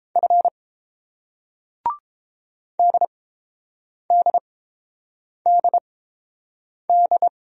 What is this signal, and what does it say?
Morse code, heard as a keyed tone, about 30 wpm, spaced as for 7 wpm, 700 Hz. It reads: F DDDD